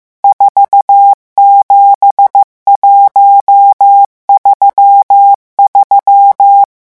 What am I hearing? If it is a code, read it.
Morse code, heard as a keyed tone, about 15 wpm, 800 Hz. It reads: 47133
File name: morse mp3 / revision11 chiffres vitesse 14